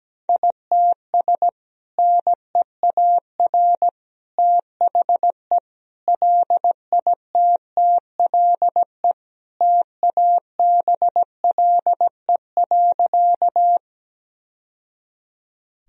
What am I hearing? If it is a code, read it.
Morse code, heard as a keyed tone, about 17 wpm, 700 Hz. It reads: ITS NEAR THE LITTLE TABLE.
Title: Sentences from Top 1000 Words - Review Part 2 - 17wpm